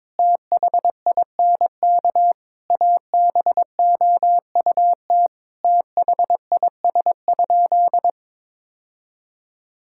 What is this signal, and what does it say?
Morse code, heard as a keyed tone, about 22 wpm, 700 Hz. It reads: THINK ABOUT THIS?
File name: practice / 22wpm / Sentences with 4-Word Phrases - 22wpm